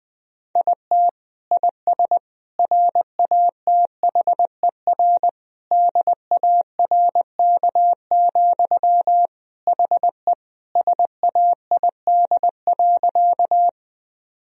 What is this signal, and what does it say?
Morse code, heard as a keyed tone, about 20 wpm, 700 Hz. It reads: IT IS RATHER DARK, HE SAID.
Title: Sentences from Top 1000 Words - Review Part 4 - 20wpm